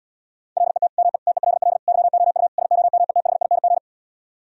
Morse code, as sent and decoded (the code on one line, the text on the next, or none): .-.. - / -- . / - . .-.. .-.. / -.-- --- ..- / ... --- -- . - .... .. -. --.
LET ME TELL YOU SOMETHING